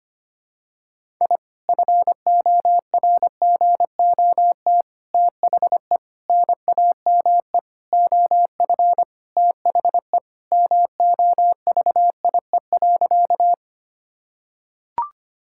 Morse code, heard as a keyed tone, about 25 wpm, 700 Hz. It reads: I FORGOT THE NAME OF THE MOVIE.
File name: practice / 25wpm / Sentences with 4-Word Phrases - 25wpm